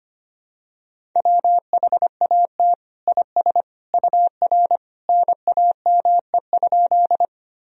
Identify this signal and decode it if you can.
Morse code, heard as a keyed tone, about 25 wpm, 700 Hz. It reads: WHAT IS UR NAME?
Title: Sentences from Top 200 Words and Abbreviations 25wpm